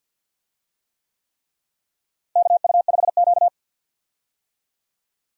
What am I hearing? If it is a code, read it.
Morse code, heard as a keyed tone, about 47 wpm, 700 Hz. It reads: KU5Q
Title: USA Call Signs - 4 Characters - Repeated 3x - 50wpm